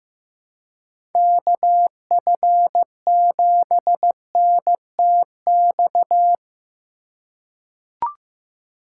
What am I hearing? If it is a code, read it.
Morse code, heard as a keyed tone, about 16 wpm, 700 Hz. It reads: KF7NTX
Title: USA Call Signs - 6 Characters - 3x Character Spacing - 15wpm